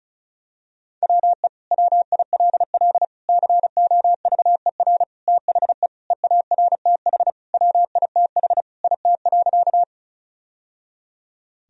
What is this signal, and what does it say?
Morse code, heard as a keyed tone, about 34 wpm, 700 Hz. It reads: WE WILL COVER THE EARTH WITH IT.
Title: Sentences from Top 1000 Words - Review Part 1 - 35wpm